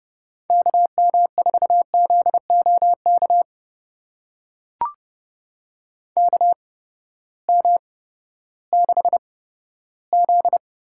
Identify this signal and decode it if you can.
Morse code, heard as a keyed tone, about 30 wpm, spaced as for 15 wpm, 700 Hz. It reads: KM4ZOK KM6Z
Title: USA Call Signs - 6 Characters - 8x Character Spacing - 30wpm